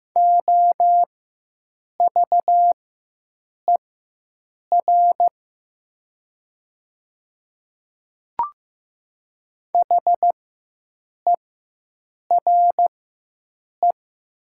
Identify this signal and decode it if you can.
Morse code, heard as a keyed tone, about 15 wpm, spaced as for 7 wpm, 700 Hz. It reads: OVER HERE
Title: 4x Character Spacing - Top 200 Words - 15wpm